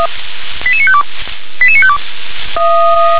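Five-tone selective call (ZVEI), heard as quick runs of five tones, tones of about 70 ms. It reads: 88952 88952
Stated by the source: Five-tone ZVEI selective call received on an SDR.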